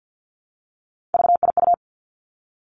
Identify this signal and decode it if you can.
Morse code, heard as a keyed tone, about 50 wpm, 700 Hz. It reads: VIU
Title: Three Challenging Letter-Number Combinations 50wpm